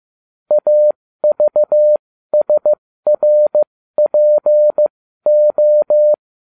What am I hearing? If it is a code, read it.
Morse code, heard as a keyed tone, about 15 wpm, spaced as for 13 wpm, 600 Hz. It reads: AVSRPO